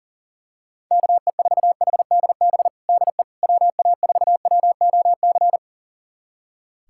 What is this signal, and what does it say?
Morse code, heard as a keyed tone, about 40 wpm, 700 Hz. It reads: KE4HDB DE WA4WOC
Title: USA Callsign DE Callsign - 6 Characters - 40wpm